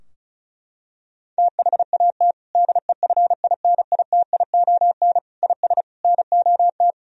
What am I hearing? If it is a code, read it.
Morse code, heard as a keyed tone, about 36 wpm, 700 Hz. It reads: THAT DEFINITION IS NOT
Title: Sentences from Top 1000 Words - Review Part 4 - 35wpm